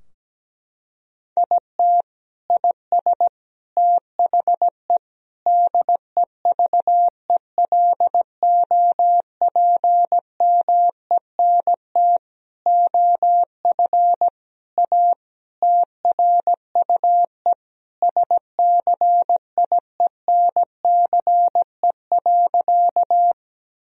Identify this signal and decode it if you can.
Morse code, heard as a keyed tone, about 17 wpm, 700 Hz. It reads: IT IS THE DEVELOPMENT OF A TRUE SCIENCE.
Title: Sentences from Top 1000 Words - Review Part 4 - 17wpm